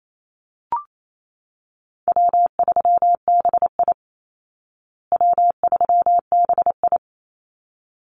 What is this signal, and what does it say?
Morse code, heard as a keyed tone, about 28 wpm, 700 Hz. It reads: W3BI W3BI
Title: Easy Call Signs Repeated 2x - 28wpm